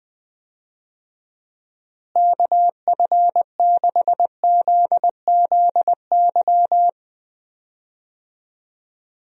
Morse code, heard as a keyed tone, about 21 wpm, 700 Hz. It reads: KF6ZZY
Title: USA Call Signs - 6 Characters - Repeated 3x - 20wpm